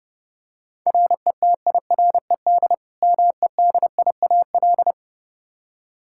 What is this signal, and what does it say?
Morse code, heard as a keyed tone, about 32 wpm, 700 Hz. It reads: RETIRED MEDIAL
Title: Practice QSO - Exchange 3 - More Info - 30wpm